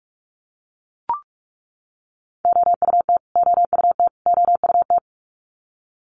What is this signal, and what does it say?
Morse code, heard as a keyed tone, about 44 wpm, 700 Hz. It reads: OUT OUT OUT
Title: Top 300 Words Repeated 45wpm